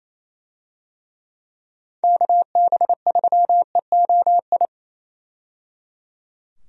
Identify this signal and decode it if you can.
Morse code, heard as a keyed tone, about 28 wpm, 700 Hz. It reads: KB3EOI
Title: USA Call Signs - 6 Characters - 28wpm